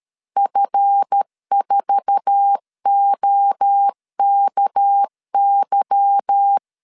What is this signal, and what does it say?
Morse code, heard as a keyed tone, about 13 wpm, 800 Hz. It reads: F4OKY
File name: CW / DET / 1 9 1500C S C 12wpm 3811